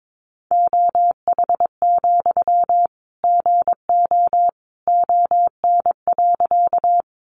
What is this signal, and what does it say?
Morse code, heard as a keyed tone, about 22 wpm, 700 Hz. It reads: OH, GO ON.
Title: Sentences from Top 500 Words 22wpm